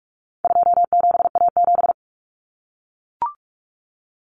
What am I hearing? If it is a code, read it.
Morse code, heard as a keyed tone, about 44 wpm, 700 Hz. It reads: 27A7
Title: Common Cut Numbers - 4-Digit 45wpm